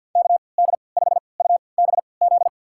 Morse code, heard as a keyed tone, about 50 wpm, spaced as for 28 wpm, 700 Hz. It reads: KD5UBZ
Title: USA Call Signs - 6 Characters - 3x Character Spacing - 50wpm